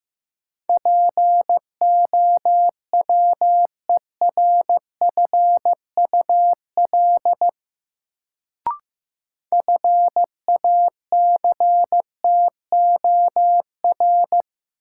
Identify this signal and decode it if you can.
Morse code, heard as a keyed tone, about 15 wpm, 700 Hz. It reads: POWERFUL FACTOR
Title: Sentences from Top 1300 Words - New Word Drill Alt - 15wpm